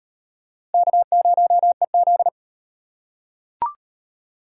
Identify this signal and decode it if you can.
Morse code, heard as a keyed tone, about 38 wpm, 700 Hz. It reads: K0EZ